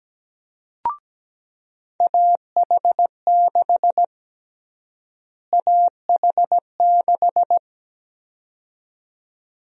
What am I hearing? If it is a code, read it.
Morse code, heard as a keyed tone, about 17 wpm, 700 Hz. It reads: AH6 AH6
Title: USA Call Sign Prefixes - Repeated 2x - 17wpm